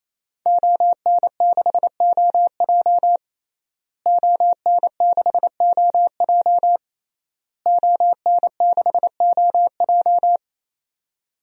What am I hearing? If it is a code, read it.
Morse code, heard as a keyed tone, about 28 wpm, 700 Hz. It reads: ON6OJ ON6OJ ON6OJ
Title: DXCC-100 Call Signs - 5 Characters - Repeated 3x - 28wpm